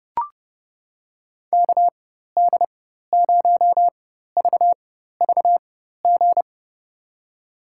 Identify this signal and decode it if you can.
Morse code, heard as a keyed tone, about 30 wpm, spaced as for 14 wpm, 700 Hz. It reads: KD0VVG
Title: USA Call Signs - 6 Characters - 4x Character Spacing - 30wpm